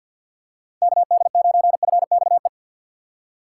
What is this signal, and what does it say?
Morse code, heard as a keyed tone, about 51 wpm, 700 Hz. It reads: KD9FXE
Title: USA Call Signs - 6 Characters - 4x Character Spacing - 50wpm